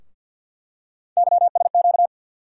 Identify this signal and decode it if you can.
Morse code, heard as a keyed tone, about 50 wpm, 700 Hz. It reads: YSQ
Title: USA Call Sign Postfixes - 3 Letters 50wpm